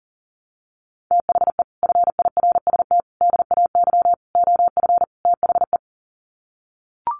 Morse code, heard as a keyed tone, about 37 wpm, 700 Hz. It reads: THE FIRST DAY OF THE
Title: Sets of 5 Words Encore2 40wpm